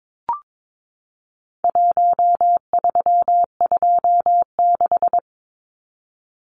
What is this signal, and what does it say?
Morse code, heard as a keyed tone, about 22 wpm, 700 Hz. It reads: 1326
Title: Numbers - 4-Digit 22wpm